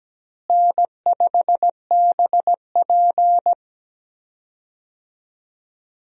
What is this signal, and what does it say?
Morse code, heard as a keyed tone, about 17 wpm, 700 Hz. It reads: N5BP
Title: USA Call Signs - 4 Characters - Repeated 3x - 17wpm